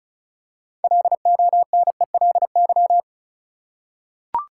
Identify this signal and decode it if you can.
Morse code, heard as a keyed tone, about 36 wpm, 700 Hz. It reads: LONELY
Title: Six Letter Words - 35wpm